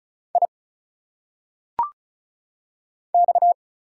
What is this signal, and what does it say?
Morse code, heard as a keyed tone, about 36 wpm, 700 Hz. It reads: I X
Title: Single Letters - Phonetic Pronunciation 35wpm